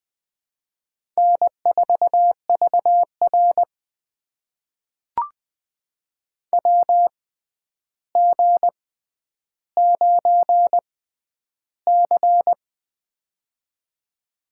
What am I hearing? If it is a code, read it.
Morse code, heard as a keyed tone, about 21 wpm, spaced as for 10 wpm, 700 Hz. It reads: N4VR WG9C
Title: USA Call Signs - 4 Characters - 6x Character Spacing - 20wpm